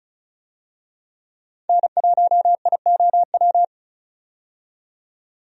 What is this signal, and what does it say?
Morse code, heard as a keyed tone, about 36 wpm, 700 Hz. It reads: N1IOW
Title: USA Call Signs - 5 Characters - 35wpm